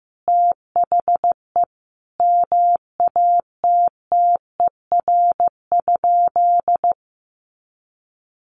Morse code, heard as a keyed tone, about 15 wpm, 700 Hz. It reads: THE MATTER?
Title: Sentences from Top 1100 Words 15wpm